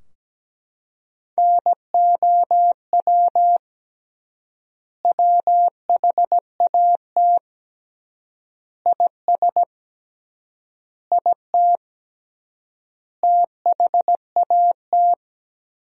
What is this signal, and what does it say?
Morse code, heard as a keyed tone, about 17 wpm, 700 Hz. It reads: NOW WHAT IS IT THAT
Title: 3x Word-Spacing - Sentences from Top 200 Words - 17wpm